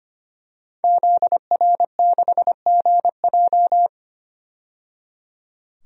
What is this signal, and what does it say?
Morse code, heard as a keyed tone, about 25 wpm, 700 Hz. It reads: ZR6GJ
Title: DXCC-100 Call Signs - 5 Characters - 25wpm